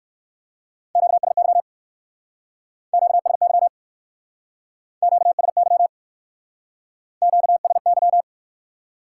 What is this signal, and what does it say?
Morse code, heard as a keyed tone, about 56 wpm, 700 Hz. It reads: QSY QSY QSY QSY
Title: Speed-Racing Top 90 QSO Elements 45wpm